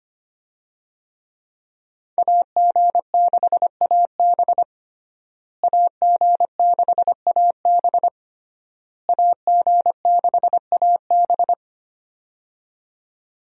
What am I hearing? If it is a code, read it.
Morse code, heard as a keyed tone, about 25 wpm, 700 Hz. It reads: AG6AB AG6AB AG6AB
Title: USA Call Signs - 5 Characters - Repeated 3x - 25wpm